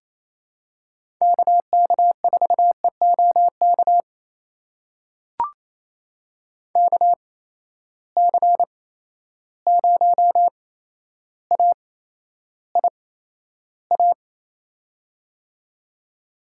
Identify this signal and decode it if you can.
Morse code, heard as a keyed tone, about 28 wpm, spaced as for 12 wpm, 700 Hz. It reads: KK4EOK KC0AIA
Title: USA Call Signs - 6 Characters - 8x Character Spacing - 28wpm